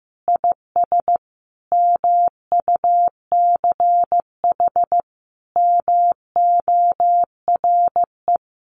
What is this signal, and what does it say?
Morse code, heard as a keyed tone, about 15 wpm, 700 Hz. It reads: IS MUCH MORE
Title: Sentences from Top 1200 Words 15wpm